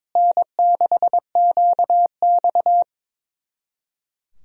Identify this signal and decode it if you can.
Morse code, heard as a keyed tone, about 22 wpm, 700 Hz. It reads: N6QX
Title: Speed-Racing Common QSO Phrases - 22wpm